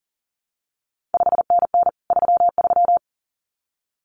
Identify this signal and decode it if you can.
Morse code, heard as a keyed tone, about 40 wpm, 700 Hz. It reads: 5NN 33
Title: CQ WW DX Contest Exchange Encore - 40wpm